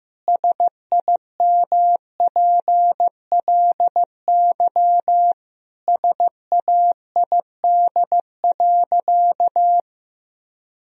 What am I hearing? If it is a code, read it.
Morse code, heard as a keyed tone, about 15 wpm, 700 Hz. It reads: SIMPLY SAID.
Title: Sentences from Top 1000 Words - Review Part 4 - 15wpm